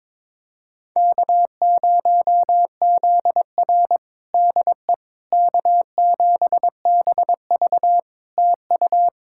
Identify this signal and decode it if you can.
Morse code, heard as a keyed tone, about 22 wpm, 700 Hz. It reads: K0ZR DE K7BV TU
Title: Practice QSO - Exchange 4 - Ending - 22wpm